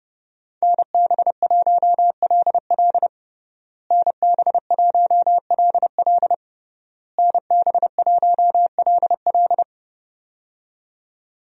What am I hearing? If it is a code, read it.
Morse code, heard as a keyed tone, about 32 wpm, 700 Hz. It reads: NB1LL NB1LL NB1LL
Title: USA Call Signs - 5 Characters - Repeated 3x - 30wpm